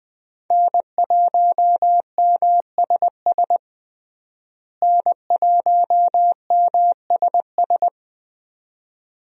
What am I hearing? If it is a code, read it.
Morse code, heard as a keyed tone, about 21 wpm, 700 Hz. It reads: N1MSS N1MSS
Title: USA Call Signs - 5 Characters - Repeated 2x - 20wpm